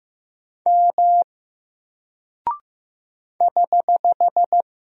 Morse code, heard as a keyed tone, about 15 wpm, 700 Hz. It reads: M <HH>
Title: Single Character (Letter:Number:Punctuation:Prosign) Alt 15wpm